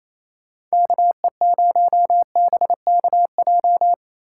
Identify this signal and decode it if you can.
Morse code, heard as a keyed tone, about 33 wpm, 700 Hz. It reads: KE0BKJ
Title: Speed Racing - USA 2x3 Call Signs 28wpm